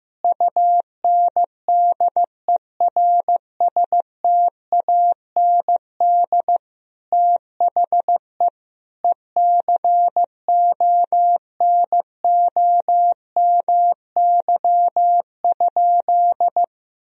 Morse code, heard as a keyed tone, about 15 wpm, 700 Hz. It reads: UNDERSTAND THE ECONOMY?
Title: Sentences from Top 1000 Words - Review Part 3 - 15wpm